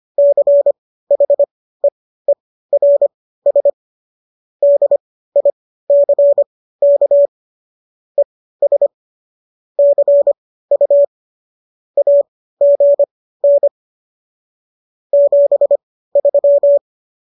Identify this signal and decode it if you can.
Morse code, heard as a keyed tone, about 25 wpm, spaced as for 15 wpm, 570 Hz. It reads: CHEERS DICK ES CU AGN 73